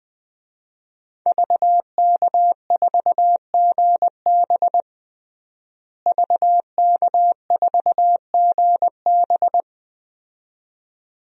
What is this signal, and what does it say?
Morse code, heard as a keyed tone, about 20 wpm, 700 Hz. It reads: VK4GB VK4GB
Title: DXCC-100 Call Signs - 5 Characters - Repeated 2x - 20wpm